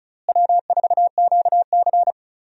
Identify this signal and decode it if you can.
Morse code, heard as a keyed tone, about 36 wpm, 700 Hz. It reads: W4QC
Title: USA Call Signs - 4 Characters - Repeated 2x - 35wpm